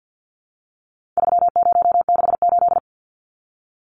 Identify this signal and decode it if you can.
Morse code, heard as a keyed tone, about 50 wpm, 700 Hz. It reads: 3068
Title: Numbers - 4-Digit 50wpm